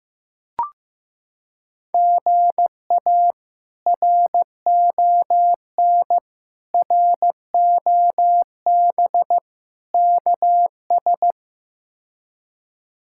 Morse code, heard as a keyed tone, about 15 wpm, 700 Hz. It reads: GA RON ROB KS
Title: K1USN SST QSO Practice - 15wpm